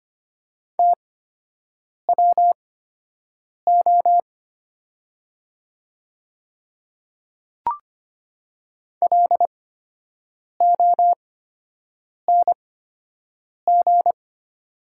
Morse code, heard as a keyed tone, about 25 wpm, spaced as for 7 wpm, 700 Hz. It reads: TWO LONG
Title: 8x Character Spacing - Top 100 Words - 25wpm